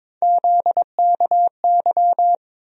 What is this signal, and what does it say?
Morse code, heard as a keyed tone, about 22 wpm, 700 Hz. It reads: ZKY
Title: USA Call Sign Postfixes - 3 Letters 22wpm